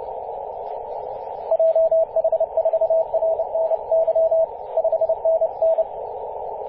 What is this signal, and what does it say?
Morse code, heard as a keyed tone, about 30 wpm, 650 Hz, with radio noise. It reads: JH4RNY 5NN
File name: 5w0xg-18-075mhz-cw